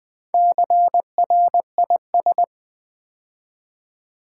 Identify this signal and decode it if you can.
Morse code, heard as a keyed tone, about 20 wpm, 700 Hz. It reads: CRIS
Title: Build A Word - 2 Letters At A Time - 6-Letter Words - 20wpm